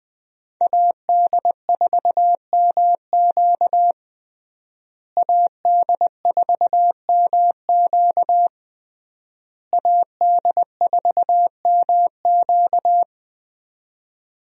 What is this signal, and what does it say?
Morse code, heard as a keyed tone, about 20 wpm, 700 Hz. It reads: AD4MQ AD4MQ AD4MQ
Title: USA Call Signs - 5 Characters - Repeated 3x - 20wpm